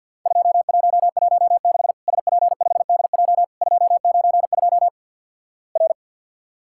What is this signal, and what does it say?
Morse code, heard as a keyed tone, about 50 wpm, 700 Hz. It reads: 2116 SP5DJ 292 R
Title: Worked All Europe DX Contest - QTC Exchange - 50wpm